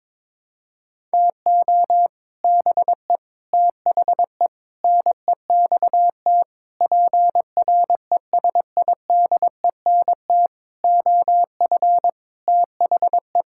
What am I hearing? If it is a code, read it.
Morse code, heard as a keyed tone, about 22 wpm, 700 Hz. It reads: TO BE THE NEXT PRESIDENT OF THE
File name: Sets of 9 Words - 22wpm